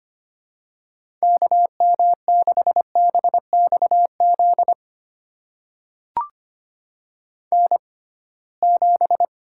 Morse code, heard as a keyed tone, about 25 wpm, spaced as for 19 wpm, 700 Hz. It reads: KM6BXZ N7
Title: USA Call Signs - 6x Character Spacing - 25wpm